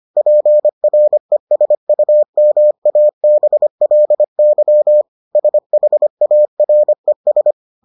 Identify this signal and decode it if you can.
Morse code, heard as a keyed tone, about 25 wpm, 600 Hz. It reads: PRESUMABLY SHARES